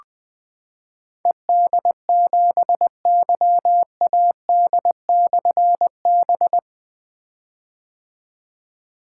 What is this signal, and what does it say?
Morse code, heard as a keyed tone, about 20 wpm, 700 Hz. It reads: ED7YAD/B
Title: Difficult Call Signs 20wpm